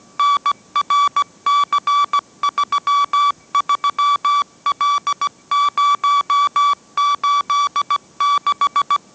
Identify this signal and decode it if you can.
Morse code, heard as a keyed tone, about 17 wpm, 1150 Hz. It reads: NRC33L086
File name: mrx / mrx080 18wpm